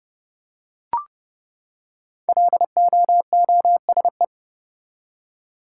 Morse code, heard as a keyed tone, about 32 wpm, 700 Hz. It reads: LOOSE
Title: Five Letter Words - 30wpm